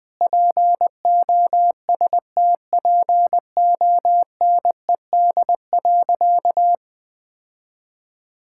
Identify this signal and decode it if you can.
Morse code, heard as a keyed tone, about 20 wpm, 700 Hz. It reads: POSTPONED.
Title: Sentences with 8-Word Phrases - 20wpm